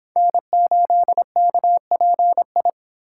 Morse code, heard as a keyed tone, about 30 wpm, 700 Hz. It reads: N8KPI